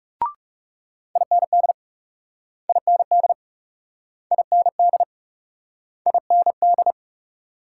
Morse code, heard as a keyed tone, about 39 wpm, 700 Hz. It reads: IND IND IND IND